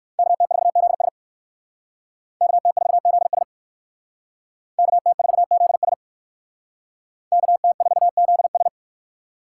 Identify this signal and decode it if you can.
Morse code, heard as a keyed tone, about 65 wpm, 700 Hz. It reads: KT4ZS KT4ZS KT4ZS KT4ZS